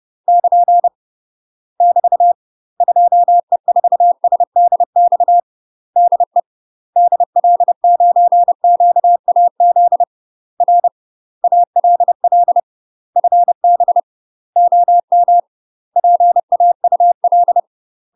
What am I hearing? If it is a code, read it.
Morse code, heard as a keyed tone, about 30 wpm, 700 Hz. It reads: <KN> <BT> 2E4SDX DE DL9QAZ R ALL FB OM PAUL
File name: UK MASTER QSOs TRAINING / 30WPM / UK MASTER QSOs 0011